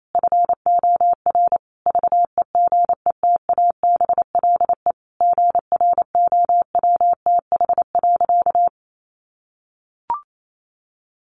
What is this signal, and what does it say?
Morse code, heard as a keyed tone, about 28 wpm, 700 Hz. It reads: FOR VEGETABLE GROWTH.
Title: Sentences from Top 1300 Words 28wpm